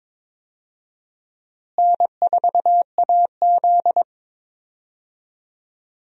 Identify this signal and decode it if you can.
Morse code, heard as a keyed tone, about 21 wpm, 700 Hz. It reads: N4AZ
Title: USA Call Signs - 4 Characters - Repeated 2x - 22wpm